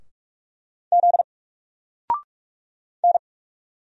Morse code, heard as a keyed tone, about 44 wpm, 700 Hz. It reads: Z N
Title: Single Character (Letter:Number:Punctuation:Prosign) 45wpm